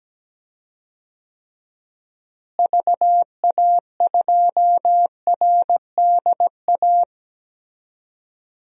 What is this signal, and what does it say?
Morse code, heard as a keyed tone, about 17 wpm, 700 Hz. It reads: VA2RDA
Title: DXCC-100 Call Signs - 6 Characters - Repeated 3x - 17wpm